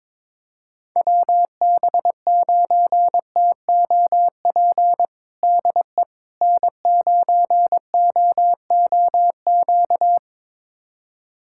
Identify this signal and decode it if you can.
Morse code, heard as a keyed tone, about 22 wpm, 700 Hz. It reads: WB9TOP DE N9OOQ
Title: USA Callsign DE Callsign 22wpm